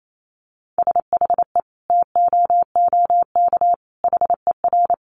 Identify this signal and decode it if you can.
Morse code, heard as a keyed tone, about 28 wpm, 700 Hz. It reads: SHE TOOK HER
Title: Sentences from Top 900 Words 28wpm